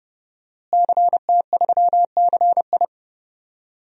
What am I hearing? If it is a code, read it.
Morse code, heard as a keyed tone, about 31 wpm, 700 Hz. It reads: CT3CI
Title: DXCC-100 Call Signs - 5 Characters - Repeated 3x - 30wpm